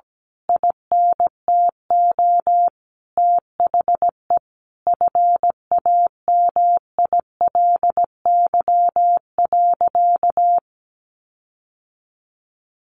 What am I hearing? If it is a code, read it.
Morse code, heard as a keyed tone, about 17 wpm, 700 Hz. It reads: INTO THE FAMILY.
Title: Sentences from Top 1500 Words 17wpm